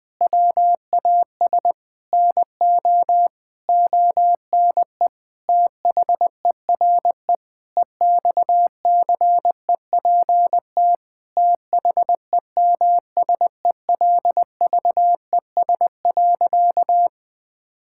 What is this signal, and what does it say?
Morse code, heard as a keyed tone, about 21 wpm, 700 Hz. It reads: WAS NO ONE THERE EXCEPT THEMSELVES.
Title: Sentences from Top 1600 Words 20wpm